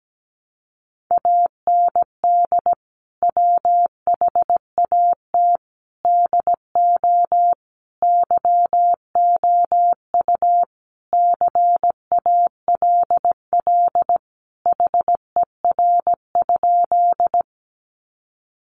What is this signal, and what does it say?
Morse code, heard as a keyed tone, about 17 wpm, 700 Hz. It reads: AND WHAT DO YOU CALL HER?
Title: Sentences from Top 100 Words 17wpm